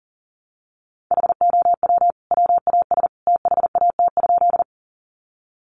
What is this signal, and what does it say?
Morse code, heard as a keyed tone, about 40 wpm, 700 Hz. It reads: HOW WAS THAT?
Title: Sentences from Top 100 Words 40wpm